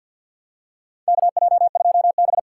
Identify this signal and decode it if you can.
Morse code, heard as a keyed tone, about 50 wpm, 700 Hz. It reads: KJ2B
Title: USA Call Signs - 4 Characters - 3x Character Spacing - 50wpm